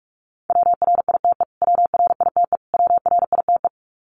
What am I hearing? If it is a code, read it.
Morse code, heard as a keyed tone, about 48 wpm, 700 Hz. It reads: WRITE WRITE WRITE
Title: Top 200 Words Repeated 45wpm